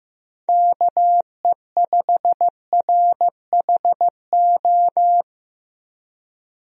Morse code, heard as a keyed tone, about 15 wpm, 700 Hz. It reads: KE5RHO
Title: USA Call Signs - 6 Characters - Repeated 3x - 15wpm